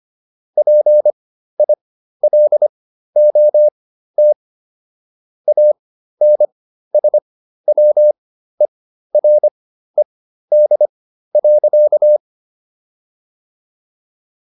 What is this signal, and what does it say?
Morse code, heard as a keyed tone, about 25 wpm, spaced as for 13 wpm, 600 Hz. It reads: PILOT ANSWERED.